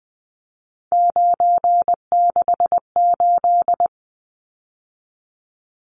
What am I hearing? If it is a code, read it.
Morse code, heard as a keyed tone, about 20 wpm, 700 Hz. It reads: 968
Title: Numbers - 3-Digit 20wpm